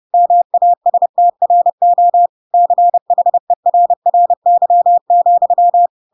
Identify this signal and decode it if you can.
Morse code, heard as a keyed tone, about 30 wpm, 700 Hz. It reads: MASTRO CHERRY,
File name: Pinocchio / Pinocchio ch1 30 30 ik1hge